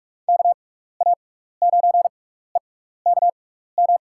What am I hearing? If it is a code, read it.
Morse code, heard as a keyed tone, about 46 wpm, 700 Hz. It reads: K A 9 E K M
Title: USA Call Signs - 6 Characters - 6x Character Spacing - 45wpm